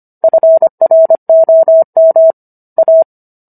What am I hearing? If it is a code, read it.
Morse code, heard as a keyed tone, about 25 wpm, 650 Hz. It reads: FROM A